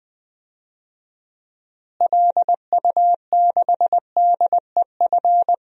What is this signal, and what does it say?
Morse code, heard as a keyed tone, about 20 wpm, 700 Hz. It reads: LU6DEF
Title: DXCC-100 Call Signs - 6 Characters - Repeated 2x - 20wpm